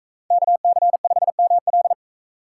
Morse code, heard as a keyed tone, about 42 wpm, 700 Hz. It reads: KC5ML